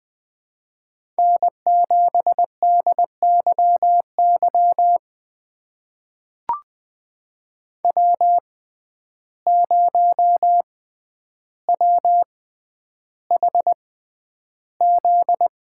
Morse code, heard as a keyed tone, about 20 wpm, spaced as for 10 wpm, 700 Hz. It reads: N7DYY W0WHZ
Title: USA Call Signs - 5 Characters - 6x Character Spacing - 20wpm